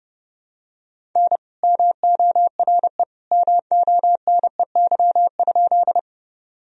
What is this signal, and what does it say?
Morse code, heard as a keyed tone, about 30 wpm, 700 Hz. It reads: N MORE MONEY?
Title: Sentences from Top 400 Words and Abbreviations 30wpm